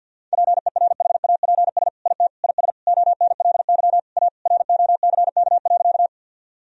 Morse code, heard as a keyed tone, about 50 wpm, 700 Hz. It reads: PERHAPS IT IS ONLY A ROCK.
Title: Sentences from Top 1000 Words - Review Part 2 - 50wpm